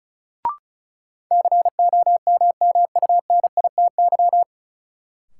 Morse code, heard as a keyed tone, about 34 wpm, 700 Hz. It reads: COMMUNITY
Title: Sentences from Top 1200 Words - New Word Drill Alt - 35wpm